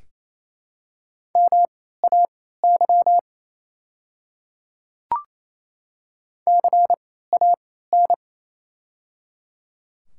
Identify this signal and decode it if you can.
Morse code, heard as a keyed tone, about 28 wpm, spaced as for 16 wpm, 700 Hz. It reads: MAY CAN
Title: 3x Character Spacing - Top 100 Words - 28wpm